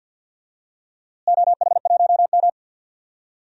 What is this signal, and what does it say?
Morse code, heard as a keyed tone, about 52 wpm, 700 Hz. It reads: OH1M
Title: DXCC-100 Call Signs - 4 Characters - Repeated 3x - 50wpm